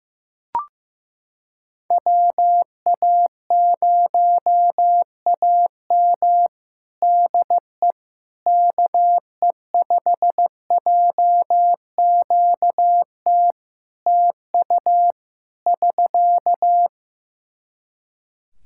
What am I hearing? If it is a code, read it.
Morse code, heard as a keyed tone, about 15 wpm, 700 Hz. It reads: WA0AM DE KE5JQT TU <SK>
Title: Practice QSO - Exchange 4 - Ending - 15wpm